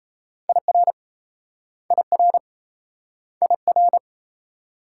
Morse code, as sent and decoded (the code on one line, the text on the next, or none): .. .-. / .. .-. / .. .-.
IR IR IR